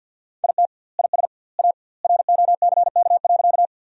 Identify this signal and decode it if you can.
Morse code, heard as a keyed tone, about 50 wpm, 700 Hz. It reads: IT IS A ROCK.
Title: 2x Word-Spacing - Sentences from Top 400 Words - 50wpm